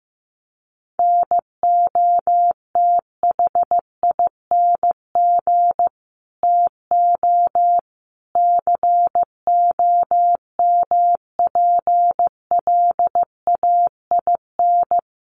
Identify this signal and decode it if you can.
Morse code, heard as a keyed tone, about 15 wpm, 700 Hz. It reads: NOTHING TO COMPLAIN
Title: Sentences from Top 1300 Words 15wpm